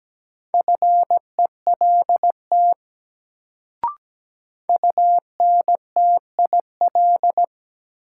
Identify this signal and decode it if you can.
Morse code, heard as a keyed tone, about 17 wpm, 700 Hz. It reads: FELT UNTIL
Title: Sets of 5 Words Encore - New Word Drill - 17wpm